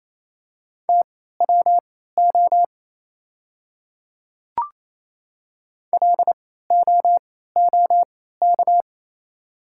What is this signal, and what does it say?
Morse code, heard as a keyed tone, about 28 wpm, spaced as for 16 wpm, 700 Hz. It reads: TWO LOOK
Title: 3x Character Spacing - Top 200 Words - 28wpm